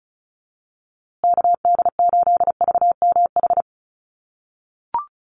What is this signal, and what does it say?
Morse code, heard as a keyed tone, about 34 wpm, 700 Hz. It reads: KD8VMH
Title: USA Call Signs Repeated 3x 35wpm